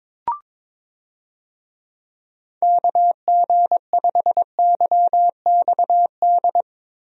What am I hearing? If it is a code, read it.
Morse code, heard as a keyed tone, about 21 wpm, 700 Hz. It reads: KG5YXD
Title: USA Call Signs - 6 Characters - Repeated 3x - 22wpm